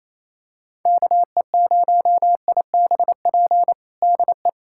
Morse code, heard as a keyed tone, about 27 wpm, 700 Hz. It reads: KE0IBP DE
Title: USA Callsign DE Callsign - 6 Characters - 28wpm